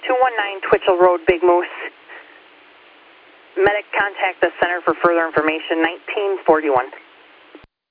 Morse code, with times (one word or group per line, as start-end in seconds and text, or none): none